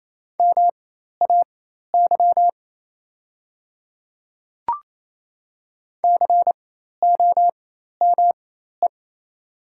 Morse code, heard as a keyed tone, about 28 wpm, spaced as for 13 wpm, 700 Hz. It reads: MAY COME
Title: 4x Character Spacing - Top 100 Words - 28wpm